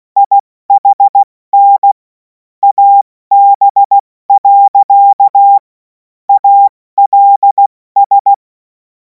Morse code, heard as a keyed tone, about 16 wpm, 800 Hz. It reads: IHN AB. ALS
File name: Morsekurs / Witze / Witz 05. Mai 2026 70 F80